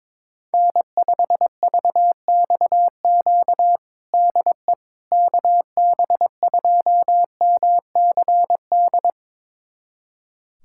Morse code, heard as a keyed tone, about 22 wpm, 700 Hz. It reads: N5VXQ DE KB2MCD
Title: USA Callsign DE Callsign 22wpm